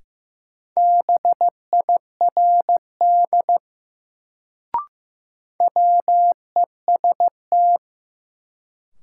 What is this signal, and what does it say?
Morse code, heard as a keyed tone, about 15 wpm, 700 Hz. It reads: BIRD WEST
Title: Sentences from Top 400 Words - New Word Drill - 15wpm